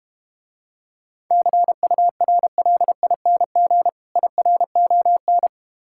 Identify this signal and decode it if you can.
Morse code, heard as a keyed tone, about 32 wpm, 700 Hz. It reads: CURLING IRON